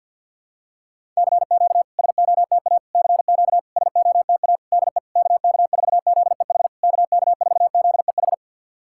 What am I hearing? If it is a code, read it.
Morse code, heard as a keyed tone, about 50 wpm, 690 Hz. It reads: CQ SOTA CQ SOTA DE KK4ZEH KK4ZEH
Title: SOTA QSO Practice 50wpm